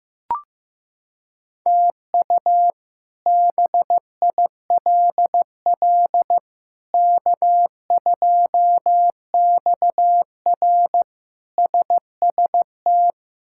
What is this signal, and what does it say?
Morse code, heard as a keyed tone, about 15 wpm, 700 Hz. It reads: TU BILL K2XR SST
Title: K1USN SST QSO Practice - 15wpm